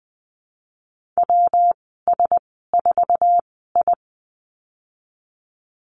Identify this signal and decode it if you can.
Morse code, heard as a keyed tone, about 20 wpm, spaced as for 15 wpm, 700 Hz. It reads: WS4I
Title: 128 Lesson 21g - Number 6 - Call Signs 20wpm